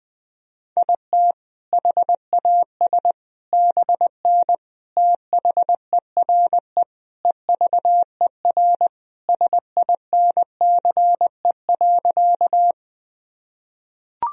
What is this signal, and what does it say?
Morse code, heard as a keyed tone, about 20 wpm, 700 Hz. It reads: IT HAS BN THERE EVER SINCE.
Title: Sentences from Top 400 Words and Abbreviations 20wpm